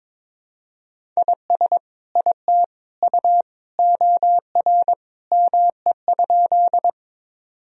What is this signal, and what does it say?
Morse code, heard as a keyed tone, about 22 wpm, 700 Hz. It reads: IS IT U OR ME?
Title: Sentences from Top 200 Words and Abbreviations 22wpm